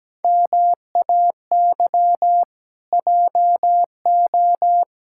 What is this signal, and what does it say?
Morse code, heard as a keyed tone, about 17 wpm, 700 Hz. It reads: MAY JO
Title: Pangram Challenge v2 - 17wpm